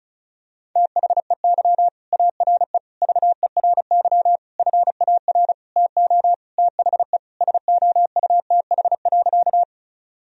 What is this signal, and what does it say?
Morse code, heard as a keyed tone, about 34 wpm, 700 Hz. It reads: THEY ARE VERY FAR TO THE SOUTH.
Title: Sentences from Top 1000 Words - Review Part 2 - 35wpm